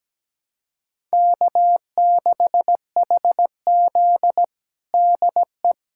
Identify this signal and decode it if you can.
Morse code, heard as a keyed tone, about 17 wpm, 700 Hz. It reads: K6HZ DE
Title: USA Callsign DE Callsign - 4 Characters - 17wpm